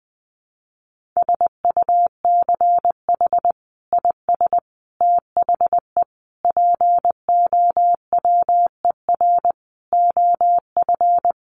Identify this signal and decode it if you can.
Morse code, heard as a keyed tone, about 20 wpm, 700 Hz. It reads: SUCH IS THE POWER OF
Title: Sentences from Top 1400 Words 20wpm